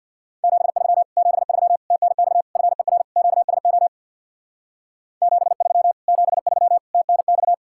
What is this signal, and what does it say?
Morse code, heard as a keyed tone, about 67 wpm, 700 Hz. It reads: 73 73 TNX FER QSO 73 73 TNX